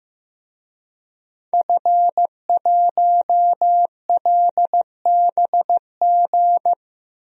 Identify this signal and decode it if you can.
Morse code, heard as a keyed tone, about 16 wpm, 700 Hz. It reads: F1LBG
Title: DXCC-100 Call Signs - 5 Characters - 15wpm